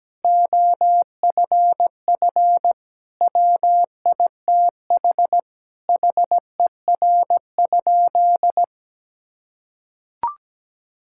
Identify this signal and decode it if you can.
Morse code, heard as a keyed tone, about 17 wpm, 700 Hz. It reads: OFF WITH HER?
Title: Sentences from Top 1000 Words - Review Part 6 - 17wpm